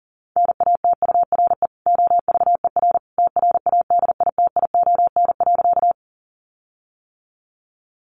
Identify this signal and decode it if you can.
Morse code, heard as a keyed tone, about 40 wpm, 700 Hz. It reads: NATURE OVER TRADITION.
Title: Sentences from Top 1500 Words 40wpm